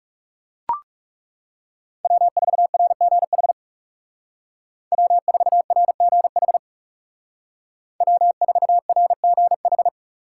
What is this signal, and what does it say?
Morse code, heard as a keyed tone, about 37 wpm, 700 Hz. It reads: W4RGH W4RGH W4RGH
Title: Speed Racing - USA 1x3 Call Signs 30wpm